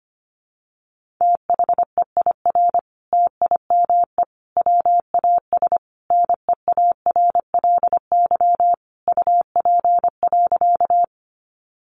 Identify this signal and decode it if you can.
Morse code, heard as a keyed tone, about 25 wpm, 700 Hz. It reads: THEIR TIME WAS NEARLY UP.
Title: Sentences from Top 1300 Words 25wpm